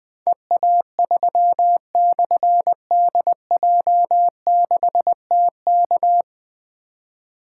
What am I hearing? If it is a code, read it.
Morse code, heard as a keyed tone, about 21 wpm, 700 Hz. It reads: EA3/DJ6TK
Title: Difficult Call Signs 20wpm